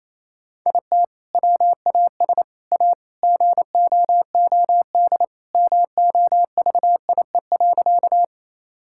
Morse code, heard as a keyed tone, about 28 wpm, 700 Hz. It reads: IT WAS A GOOD MOVIE.
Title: Sentences from Top 1000 Words - Review Part 5 - 28wpm